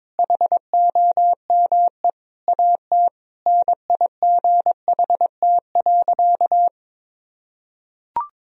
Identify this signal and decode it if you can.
Morse code, heard as a keyed tone, about 22 wpm, 700 Hz. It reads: HOME AT NIGHT.
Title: Sentences from Top 1000 Words - Review Part 3 - 22wpm